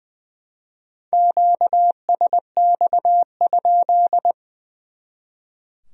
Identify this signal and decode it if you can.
Morse code, heard as a keyed tone, about 25 wpm, 700 Hz. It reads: QSX?
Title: Speed-Racing Q-Codes - 20wpm